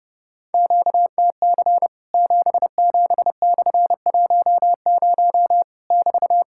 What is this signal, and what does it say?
Morse code, heard as a keyed tone, about 31 wpm, 700 Hz. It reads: QTC 77/10 <BT>
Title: Worked All Europe DX Contest - QTC Exchange - 30wpm